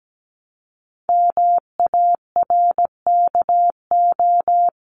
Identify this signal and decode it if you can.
Morse code, heard as a keyed tone, about 17 wpm, 700 Hz. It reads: MARKO
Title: CWT Exchange 17wpm